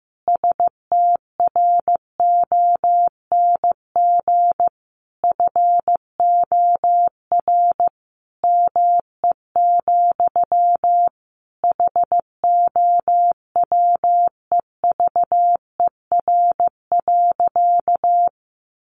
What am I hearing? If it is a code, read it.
Morse code, heard as a keyed tone, about 15 wpm, 700 Hz. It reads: STRONG FOR ME, HOWEVER.
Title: Sentences from Top 600 Words 15wpm